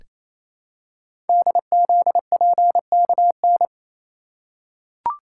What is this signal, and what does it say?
Morse code, heard as a keyed tone, about 27 wpm, 700 Hz. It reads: DZPKN
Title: Code Groups - 5 Letters - 28wpm